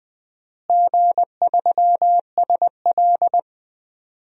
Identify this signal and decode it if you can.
Morse code, heard as a keyed tone, about 20 wpm, 700 Hz. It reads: G3SL
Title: DXCC-100 Call Signs - 4 Characters - Repeated 3x - 20wpm